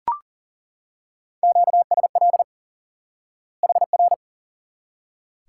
Morse code, heard as a keyed tone, about 37 wpm, 700 Hz. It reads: QSL HR
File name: IQE Course 3 - Lesson 89 - YAGI - Two Elements - 40wpm